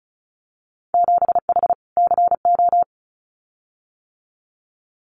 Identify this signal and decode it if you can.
Morse code, heard as a keyed tone, about 36 wpm, 700 Hz. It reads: 7H CO
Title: Winter Field Day Exchanges Encore 35wpm